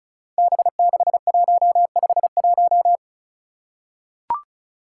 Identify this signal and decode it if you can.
Morse code, heard as a keyed tone, about 34 wpm, 700 Hz. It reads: B6151
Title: Five Challenging Letter-Number Combinations 35wpm